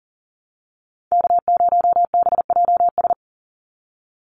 Morse code, heard as a keyed tone, about 40 wpm, 700 Hz. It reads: K0BJS
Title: USA Call Signs 40wpm